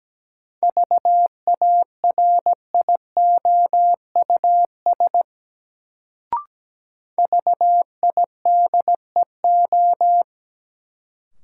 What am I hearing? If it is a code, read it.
Morse code, heard as a keyed tone, about 17 wpm, 700 Hz. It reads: VARIOUS VIDEO
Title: Sentences from Top 1000 Words - Review Part 4 - New Word Drill - 17wpm